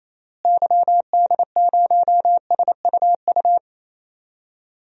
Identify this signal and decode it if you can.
Morse code, heard as a keyed tone, about 28 wpm, 700 Hz. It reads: YD0SUU
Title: DXCC-100 Call Signs - 6 Characters - Repeated 2x - 28wpm